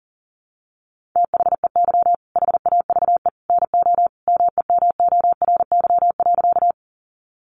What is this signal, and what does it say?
Morse code, heard as a keyed tone, about 40 wpm, 700 Hz. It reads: THEY HAVE NO MEMORY.
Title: Sentences from Top 1300 Words 40wpm